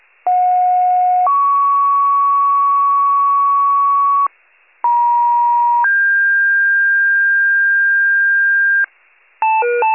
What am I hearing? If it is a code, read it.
Two-tone page, heard as a one-second tone, then a three-second tone, about 730 then 1120 Hz, about 950 then 1600 Hz, with noise. It is heard from 0.3 to 8.8 s.
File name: wcalert